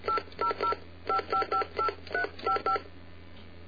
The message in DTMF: 8005558355